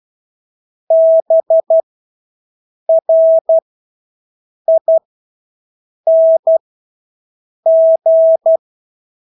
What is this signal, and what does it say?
Morse code, heard as a keyed tone, about 12 wpm, spaced as for 6 wpm, 650 Hz. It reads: BRING
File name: Engelske ord 12-6 wpm